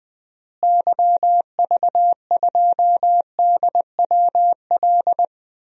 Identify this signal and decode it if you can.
Morse code, heard as a keyed tone, about 21 wpm, 700 Hz. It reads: YV2DWL
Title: DXCC-100 Call Signs - 6 Characters - Repeated 3x - 20wpm